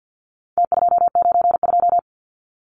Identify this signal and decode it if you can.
Morse code, heard as a keyed tone, about 50 wpm, 700 Hz. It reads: T292
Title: Numbers - Contest Style - 4-Digit 50wpm